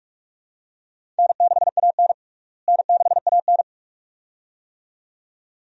Morse code, heard as a keyed tone, about 46 wpm, 700 Hz. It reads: N6AN N6AN
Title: USA Call Signs - 4 Characters - Repeated 2x - 45wpm